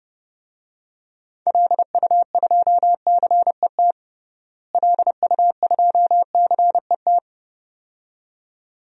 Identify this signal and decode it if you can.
Morse code, heard as a keyed tone, about 29 wpm, 700 Hz. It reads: LU2CET LU2CET
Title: DXCC-100 Call Signs - 6 Characters - Repeated 2x - 30wpm